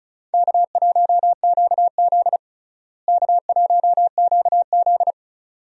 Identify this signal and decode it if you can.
Morse code, heard as a keyed tone, about 34 wpm, 700 Hz. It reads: K1QZ K1QZ
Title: USA Call Signs - 4 Characters - Repeated 2x - 35wpm